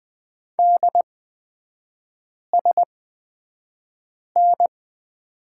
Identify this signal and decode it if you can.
Morse code, heard as a keyed tone, about 20 wpm, spaced as for 5 wpm, 690 Hz. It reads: DSN
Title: 229 Lesson 37e - Letter x - Three Characters 20wpm